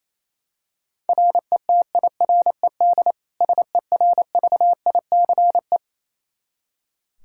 Morse code, heard as a keyed tone, about 28 wpm, 700 Hz. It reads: RETIRED SERVICE
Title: K1USN SST QSO Practice 28wpm